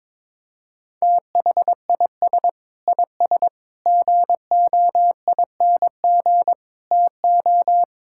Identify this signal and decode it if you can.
Morse code, heard as a keyed tone, about 22 wpm, 700 Hz. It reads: THIS IS GOING TO
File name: practice / 22wpm / Build A Phrase - 5-Word Phrases - 22wpm